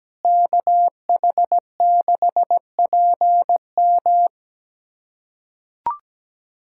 Morse code, heard as a keyed tone, about 17 wpm, 700 Hz. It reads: KH6PM
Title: USA Callsign DE Callsign - 5 Characters - 17wpm